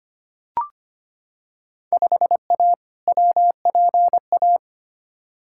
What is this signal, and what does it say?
Morse code, heard as a keyed tone, about 25 wpm, 700 Hz. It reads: 5A WPA
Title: Field Day Exchanges Encore - 25wpm